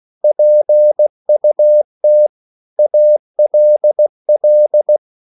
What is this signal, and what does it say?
Morse code, heard as a keyed tone, about 16 wpm, 600 Hz. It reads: PUT ALL
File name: CW / Practice / EnglishSayings16